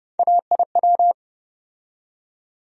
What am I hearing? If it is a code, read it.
Morse code, heard as a keyed tone, about 30 wpm, 700 Hz. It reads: AIW
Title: USA Call Sign Postfixes - 3 Letters 30wpm